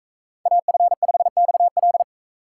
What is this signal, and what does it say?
Morse code, heard as a keyed tone, about 41 wpm, 700 Hz. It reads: AF5XL